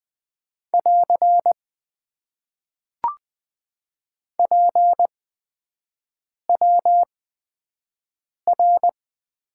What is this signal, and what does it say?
Morse code, heard as a keyed tone, about 20 wpm, spaced as for 5 wpm, 700 Hz. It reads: <AR> PWR
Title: Top 90 QSO Elements - 8x Character Spacing - 20wpm